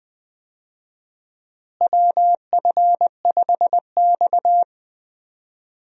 Morse code, heard as a keyed tone, about 20 wpm, 700 Hz. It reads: WF5X
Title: USA Call Signs - 4 Characters - Repeated 3x - 20wpm